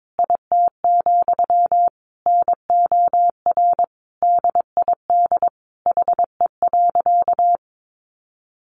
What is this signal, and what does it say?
Morse code, heard as a keyed tone, about 22 wpm, 700 Hz. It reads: IT, NOR DID HE.
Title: Sentences from Top 1500 Words 22wpm